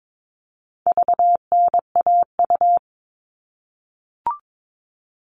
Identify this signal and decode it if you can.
Morse code, heard as a keyed tone, about 22 wpm, 700 Hz. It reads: VNAU
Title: All Cut Numbers - 4-Digit 22wpm